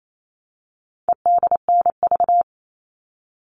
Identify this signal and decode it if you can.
Morse code, heard as a keyed tone, about 28 wpm, 700 Hz. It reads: EDNV
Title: All Cut Numbers - 4-Digit 28wpm